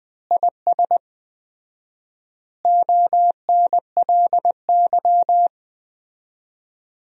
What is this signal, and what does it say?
Morse code, heard as a keyed tone, about 20 wpm, 700 Hz. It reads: IS ONLY
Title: 4x Word-Spacing - Sentences from Top 300 Words - 20wpm